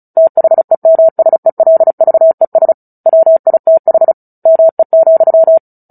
Morse code, heard as a keyed tone, about 34 wpm, 650 Hz. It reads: THEMSELVES WITH ME,
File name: Book Two / 09-Chapter 9 - 35 WPM